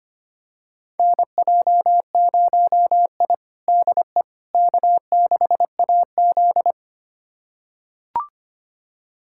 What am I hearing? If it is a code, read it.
Morse code, heard as a keyed tone, about 25 wpm, 700 Hz. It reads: NJ0I DE K6AZ
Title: USA Callsign DE Callsign - 4 Characters - 25wpm